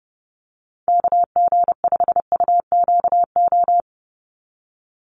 Morse code, heard as a keyed tone, about 30 wpm, 700 Hz. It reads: KG5UQO
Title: USA Call Signs 30wpm